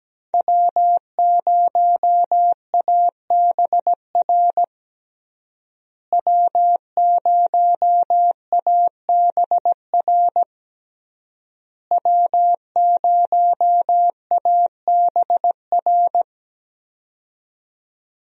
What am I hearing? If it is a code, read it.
Morse code, heard as a keyed tone, about 17 wpm, 700 Hz. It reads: W0ABR W0ABR W0ABR
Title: USA Call Signs - 5 Characters - Repeated 3x - 17wpm